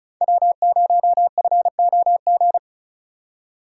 Morse code, heard as a keyed tone, about 36 wpm, 700 Hz. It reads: W0FOG
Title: USA Call Signs - 5 Characters - 3x Character Spacing - 35wpm